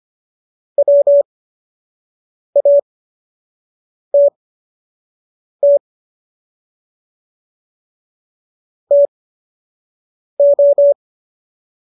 Morse code, heard as a keyed tone, about 25 wpm, spaced as for 6 wpm, 570 Hz. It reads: WATT TO